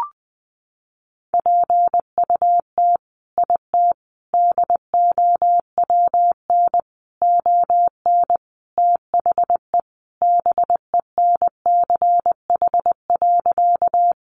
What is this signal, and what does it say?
Morse code, heard as a keyed tone, about 20 wpm, 700 Hz. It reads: PUT IT DOWN ON THE BENCH.
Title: Sentences from Top 1400 Words 20wpm